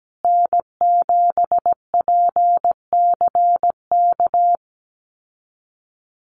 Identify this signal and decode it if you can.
Morse code, heard as a keyed tone, about 17 wpm, 700 Hz. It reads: N7PCK
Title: USA Call Signs Repeated 2x 17wpm